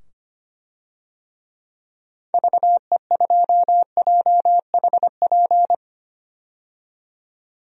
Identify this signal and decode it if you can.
Morse code, heard as a keyed tone, about 25 wpm, 690 Hz. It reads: VE2JHP
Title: DXCC-100 Call Signs - 6 Characters - 25wpm